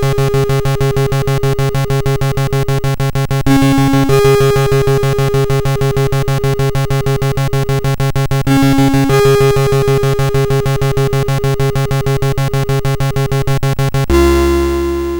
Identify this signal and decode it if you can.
There is no code.